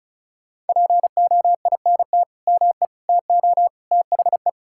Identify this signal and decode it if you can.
Morse code, heard as a keyed tone, about 36 wpm, 700 Hz. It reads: POINT ME TO THE
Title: Sentences from Top 1600 Words 35wpm